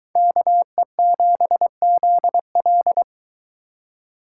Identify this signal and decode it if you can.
Morse code, heard as a keyed tone, about 23 wpm, 700 Hz. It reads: KE7ZL